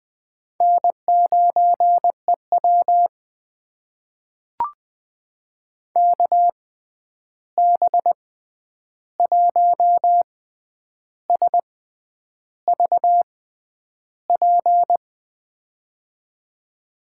Morse code, heard as a keyed tone, about 20 wpm, spaced as for 9 wpm, 700 Hz. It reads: N9EW KB1SVP
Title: USA Call Signs - 6x Character Spacing - 20wpm